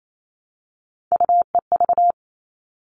Morse code, heard as a keyed tone, about 28 wpm, 700 Hz. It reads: UEV
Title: All Cut Numbers - 3-Digit 28wpm